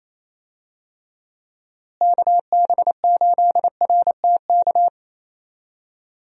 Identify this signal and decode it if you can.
Morse code, heard as a keyed tone, about 28 wpm, 700 Hz. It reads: KB8RTK
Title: USA Call Signs - 6 Characters - Repeated 2x - 28wpm